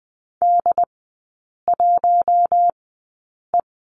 Morse code, heard as a keyed tone, about 20 wpm, spaced as for 8 wpm, 700 Hz. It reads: D1E
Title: 067 Lesson 12e - Letter d - Three Characters 20wpm